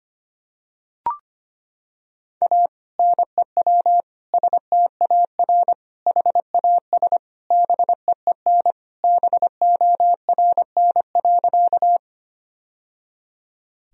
Morse code, heard as a keyed tone, about 25 wpm, 700 Hz. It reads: A NEW STAR HAS BEEN BORN.
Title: Sentences from Top 1000 Words - Review Part 2 - 25wpm